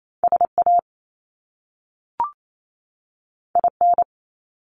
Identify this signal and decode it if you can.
Morse code, heard as a keyed tone, about 28 wpm, 700 Hz. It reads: SA IN
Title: Top Two Letter Combinations in Words 28wpm